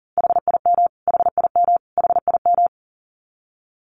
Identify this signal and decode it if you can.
Morse code, heard as a keyed tone, about 40 wpm, 700 Hz. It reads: HIM HIM HIM
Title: Top 200 Words Repeated 40wpm